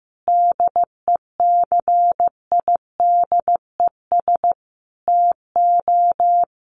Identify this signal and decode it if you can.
Morse code, heard as a keyed tone, about 14 wpm, 700 Hz. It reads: DECIDES TO
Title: Additional Easy Phrases 15wpm